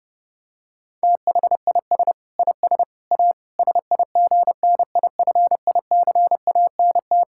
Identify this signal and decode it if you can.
Morse code, heard as a keyed tone, about 32 wpm, 700 Hz. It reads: THIS IS A SIGNIFICANT
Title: Sentences from Top 1000 Words - Review Part 6 - 30wpm